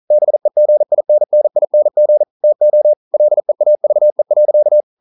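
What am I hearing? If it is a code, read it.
Morse code, heard as a keyed tone, about 40 wpm, 600 Hz. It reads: BEGINNING TO LEAVE.
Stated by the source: Morse code at 40 wpm, practice text, CW med Gnister.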